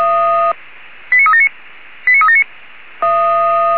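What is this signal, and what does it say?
Five-tone selective call (ZVEI), heard as quick runs of five tones, tones of about 70 ms. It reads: 89379 89379
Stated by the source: Five-tone ZVEI selective call received on SDR.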